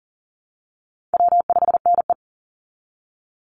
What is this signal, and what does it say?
Morse code, heard as a keyed tone, about 40 wpm, 700 Hz. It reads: W5NE
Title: Easy Call Signs Repeated 3x - 40wpm